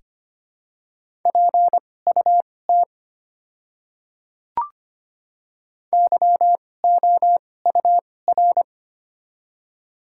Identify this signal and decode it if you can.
Morse code, heard as a keyed tone, about 25 wpm, spaced as for 18 wpm, 690 Hz. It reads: PUT YOUR
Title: 2x Character Spacing - Top 200 Words - 25wpm